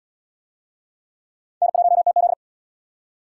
Morse code, heard as a keyed tone, about 71 wpm, 690 Hz. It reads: N1TZ